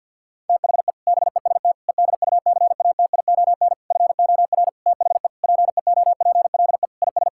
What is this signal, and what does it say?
Morse code, heard as a keyed tone, about 50 wpm, 700 Hz. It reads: THE BEST EDUCATION FOR THE PEOPLE IS
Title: Sentences from Top 1000 Words - Review Part 6 - 50wpm